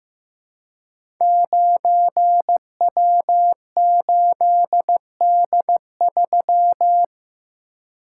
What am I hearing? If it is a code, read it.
Morse code, heard as a keyed tone, about 15 wpm, 700 Hz. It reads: 9W8D3
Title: Code Groups - 5 Letter:Numbers - 15wpm